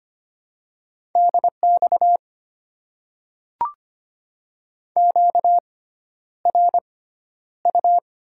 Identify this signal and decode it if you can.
Morse code, heard as a keyed tone, about 25 wpm, spaced as for 11 wpm, 700 Hz. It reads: DX QRU
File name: Top 90 QSO Elements - 6x Character Spacing - 25wpm